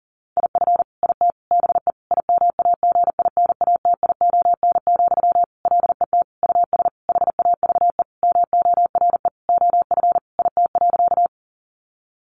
Morse code, heard as a keyed tone, about 40 wpm, 700 Hz. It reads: IF IT BE IMAGINATION, LET US HAVE MORE OF IT.
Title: Sentences from Top 1300 Words 40wpm